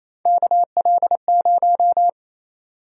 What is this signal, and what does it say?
Morse code, heard as a keyed tone, about 28 wpm, 700 Hz. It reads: KL0
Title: USA Call Sign Prefixes - 28wpm